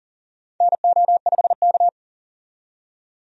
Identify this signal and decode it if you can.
Morse code, heard as a keyed tone, about 37 wpm, 700 Hz. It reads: NO5K
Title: USA Call Signs - 4 Characters - 40wpm